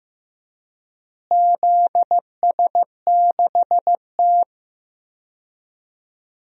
Morse code, heard as a keyed tone, about 15 wpm, 700 Hz. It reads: ZS6T
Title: DXCC-100 Call Signs - 4 Characters - Repeated 3x - 15wpm